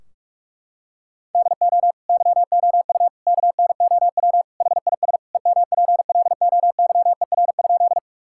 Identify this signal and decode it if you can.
Morse code, heard as a keyed tone, about 46 wpm, 700 Hz. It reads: DO YOU KNOW HIS EMPLOYER?
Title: Sentences from Top 1600 Words 45wpm